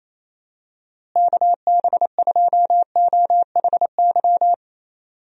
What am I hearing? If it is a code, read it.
Morse code, heard as a keyed tone, about 28 wpm, 700 Hz. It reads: KB2OHY
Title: USA Call Signs - 6 Characters - Repeated 3x - 28wpm